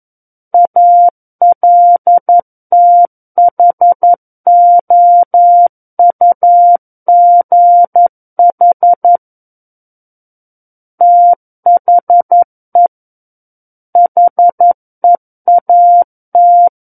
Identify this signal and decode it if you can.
Morse code, heard as a keyed tone, about 11 wpm, 700 Hz. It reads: ALTHOUGH THE HEAT
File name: 11 WPM